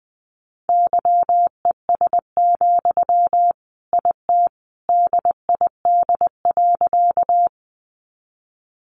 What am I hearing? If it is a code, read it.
Morse code, heard as a keyed tone, about 20 wpm, 700 Hz. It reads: YES, IT DID.
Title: Sentences from Top 500 Words 20wpm